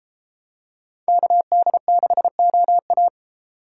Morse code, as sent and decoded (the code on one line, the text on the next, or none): -.- -.. -.... --- .-
KD6OA